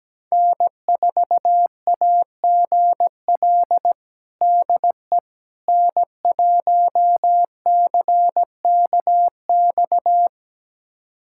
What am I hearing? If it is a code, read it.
Morse code, heard as a keyed tone, about 17 wpm, 700 Hz. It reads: N4AGL DE N1CKX
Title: USA Callsign DE Callsign - 5 Characters - 17wpm